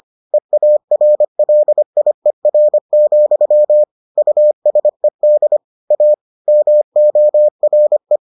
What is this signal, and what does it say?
Morse code, heard as a keyed tone, about 25 wpm, 600 Hz. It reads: EARLIER, USED A MORE